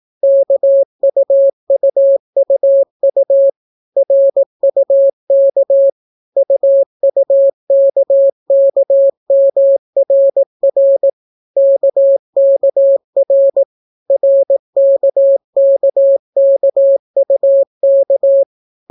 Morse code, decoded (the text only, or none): KUUUU RUK UUKKMRR KKR RKKKUK